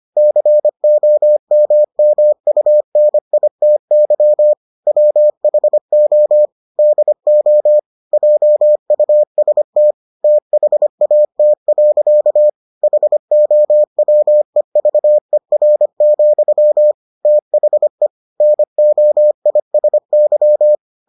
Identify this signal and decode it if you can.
Morse code, heard as a keyed tone, about 25 wpm, 600 Hz. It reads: COMMUNITY WHO DO JUST THAT. HOWEVER, THE NOISY